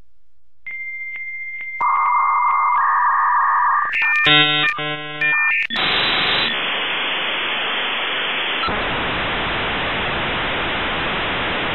Dial-up modem handshake.